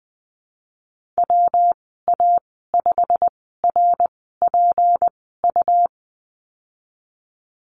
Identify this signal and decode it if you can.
Morse code, heard as a keyed tone, about 20 wpm, spaced as for 15 wpm, 700 Hz. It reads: WA5RPU
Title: 154 Lesson 25g - Letter p - Call Signs 20wpm